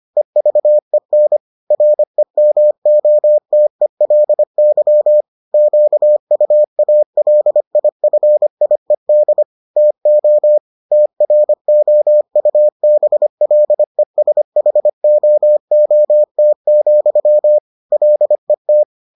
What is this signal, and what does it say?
Morse code, heard as a keyed tone, about 25 wpm, 600 Hz. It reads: EVEN REMOTELY QUALIFIED TO TROUBLESHOOT, LET